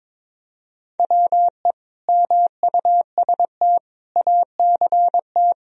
Morse code, heard as a keyed tone, about 22 wpm, 700 Hz. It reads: WE MUST ACT
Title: Sentences from Top 1000 Words - Review Part 1 - 22wpm